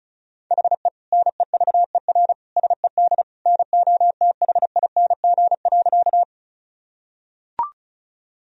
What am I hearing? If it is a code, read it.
Morse code, heard as a keyed tone, about 34 wpm, 700 Hz. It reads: HE NEVER SED NOTHING.
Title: Sentences from Top 500 Words and Abbreviations 35wpm